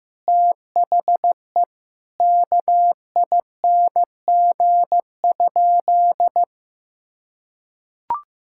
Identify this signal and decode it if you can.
Morse code, heard as a keyed tone, about 16 wpm, 700 Hz. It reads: THE KING?
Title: Sentences from Top 1000 Words - Review Part 6 - 15wpm